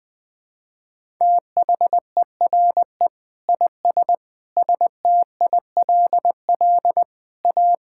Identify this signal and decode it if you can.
Morse code, heard as a keyed tone, about 20 wpm, 700 Hz. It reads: THERE IS STILL A
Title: Sets of 9 Words - 20wpm